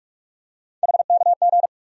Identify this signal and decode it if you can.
Morse code, heard as a keyed tone, about 43 wpm, 700 Hz. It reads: HKG
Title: USA Call Sign Postfixes - 3 Letters 45wpm